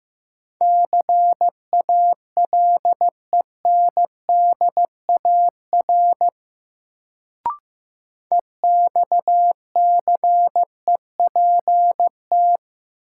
Morse code, heard as a keyed tone, about 15 wpm, 700 Hz. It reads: CALENDAR EXCEPT
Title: Sentences from Top 1600 Words - New Word Drill - 15wpm